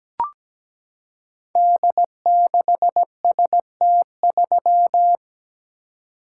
Morse code, heard as a keyed tone, about 17 wpm, 700 Hz. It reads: D6ST3
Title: Code Groups - 5 Letter:Numbers - 17wpm